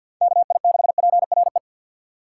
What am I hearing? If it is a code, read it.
Morse code, heard as a keyed tone, about 45 wpm, 700 Hz. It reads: KI6PRE
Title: USA Call Signs - 4x Character Spacing - 50wpm